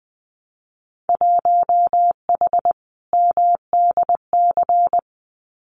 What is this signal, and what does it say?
Morse code, heard as a keyed tone, about 20 wpm, 700 Hz. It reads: 1H MDC
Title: Winter Field Day Exchanges 20wpm